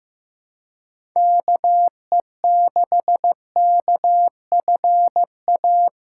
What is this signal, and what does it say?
Morse code, heard as a keyed tone, about 16 wpm, 700 Hz. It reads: KE6KFA
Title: USA Callsign DE Callsign - 6 Characters - 15wpm